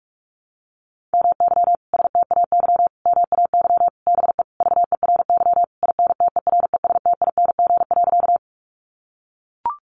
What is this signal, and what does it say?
Morse code, heard as a keyed tone, about 44 wpm, 700 Hz. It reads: MY STAY MAY BE VERY INTERESTING.
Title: Sentences from Top 1000 Words 45wpm